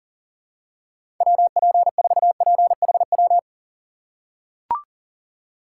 Morse code, heard as a keyed tone, about 40 wpm, 700 Hz. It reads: WP4PHW
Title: USA Call Signs - 6 Characters - 2x Character Spacing - 40wpm